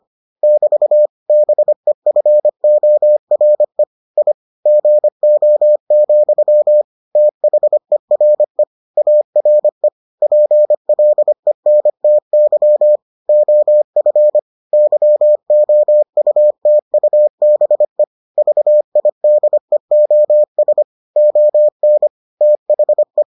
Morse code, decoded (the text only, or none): <BT> BEFORE I GO, THERE ARE PLENTY OF YOUTUBE VIDEOS ON THE